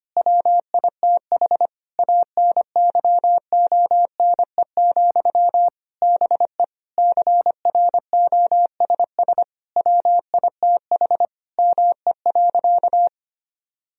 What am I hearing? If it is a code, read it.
Morse code, heard as a keyed tone, about 25 wpm, 700 Hz. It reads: WITH ANYONE, BE CROSS WITH ME.
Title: Sentences from Top 1000 Words - Review Part 1 - 25wpm